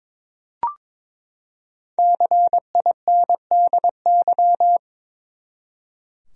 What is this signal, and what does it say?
Morse code, heard as a keyed tone, about 22 wpm, 700 Hz. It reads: CINDY
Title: USA Common Names - 22wpm